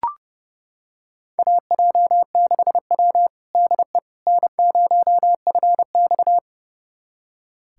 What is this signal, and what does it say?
Morse code, heard as a keyed tone, about 32 wpm, 700 Hz. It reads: AJ6W DE N0FX
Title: USA Callsign DE Callsign - 4 Characters - 30wpm